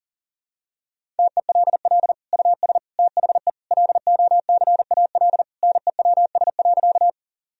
Morse code, heard as a keyed tone, about 40 wpm, 700 Hz. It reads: TELL US THE LOCAL NEWS.
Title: Sentences from Top 1000 Words - Review Part 3 - 40wpm